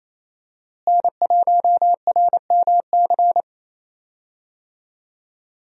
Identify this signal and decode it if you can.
Morse code, heard as a keyed tone, about 28 wpm, 700 Hz. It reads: N1RMC
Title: USA Call Signs - 5 Characters - Repeated 3x - 28wpm